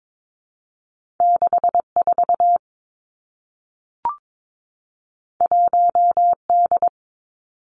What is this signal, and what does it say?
Morse code, heard as a keyed tone, about 22 wpm, 700 Hz. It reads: 6 4 1 D
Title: Two Challenging Letter-Number Combinations 22wpm